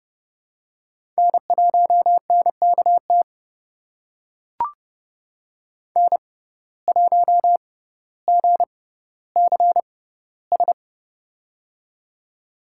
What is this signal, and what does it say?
Morse code, heard as a keyed tone, about 30 wpm, spaced as for 15 wpm, 700 Hz. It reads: N1NKT N1GCS
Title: USA Call Signs - 5 Characters - 6x Character Spacing - 30wpm